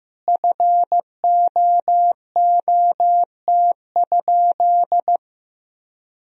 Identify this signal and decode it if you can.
Morse code, heard as a keyed tone, about 16 wpm, 700 Hz. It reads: FOOT?
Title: Sentences from Top 1000 Words - Review Part 3 - 15wpm